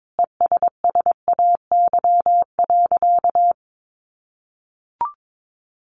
Morse code, heard as a keyed tone, about 22 wpm, 700 Hz. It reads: ESSAY.
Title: Sentences from Top 1500 Words 22wpm